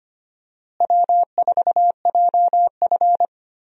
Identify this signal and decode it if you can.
Morse code, heard as a keyed tone, about 25 wpm, 700 Hz. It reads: W4JF
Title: USA Call Signs - 4 Characters - Repeated 3x - 25wpm